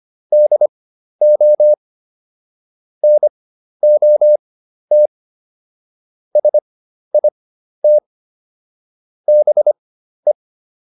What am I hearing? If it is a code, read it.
Morse code, heard as a keyed tone, about 25 wpm, spaced as for 12 wpm, 600 Hz. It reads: DO NOT SIT BE